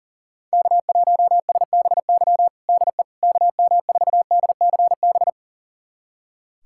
Morse code, heard as a keyed tone, about 37 wpm, 700 Hz. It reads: K1SBY DE KM4DCB
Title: USA Callsign DE Callsign 40wpm